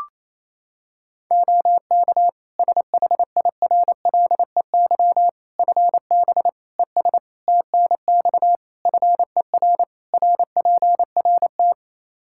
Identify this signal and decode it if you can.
Morse code, heard as a keyed tone, about 27 wpm, 700 Hz. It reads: OK SHIRLEY FB ES TNX FER RPRT
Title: Practice QSO - Exchange 2 - Info - 28wpm